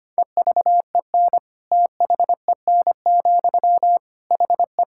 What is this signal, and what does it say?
Morse code, heard as a keyed tone, about 25 wpm, 700 Hz. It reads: EVEN THEN, HE
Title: Sentences from Top 1600 Words 25wpm